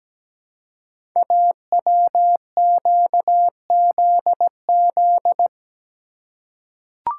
Morse code, heard as a keyed tone, about 17 wpm, 700 Hz. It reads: AWQZZ
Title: Code Groups - 5 Letters - 17wpm